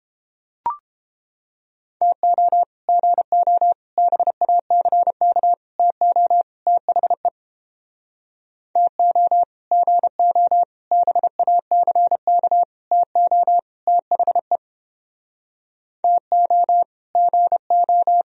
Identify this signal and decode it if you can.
Morse code, heard as a keyed tone, about 29 wpm, 700 Hz. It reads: TO GO BACK TO THE TO GO BACK TO THE TO GO